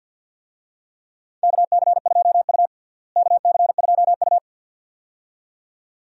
Morse code, heard as a keyed tone, about 47 wpm, 700 Hz. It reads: KC2U KC2U
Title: USA Call Signs - 4 Characters - Repeated 2x - 50wpm